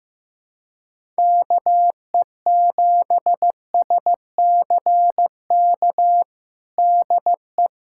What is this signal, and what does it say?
Morse code, heard as a keyed tone, about 15 wpm, 700 Hz. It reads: KE7SCK DE
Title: USA Callsign DE Callsign - 6 Characters - 15wpm